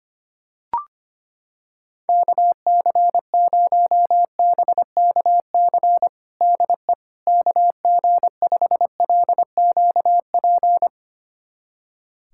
Morse code, heard as a keyed tone, about 25 wpm, 700 Hz. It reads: KC0BKC DE KG5LQP
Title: USA Callsign DE Callsign - 6 Characters - 25wpm